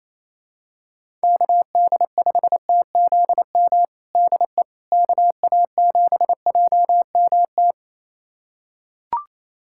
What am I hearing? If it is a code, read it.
Morse code, heard as a keyed tone, about 28 wpm, 700 Hz. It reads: KD5TZM DE KA7JMT
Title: USA Callsign DE Callsign 28wpm